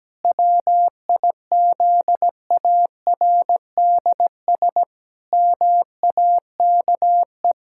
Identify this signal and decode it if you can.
Morse code, heard as a keyed tone, about 17 wpm, 700 Hz. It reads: WIZARDS MAKE
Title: Pangram Challenge v2 - 17wpm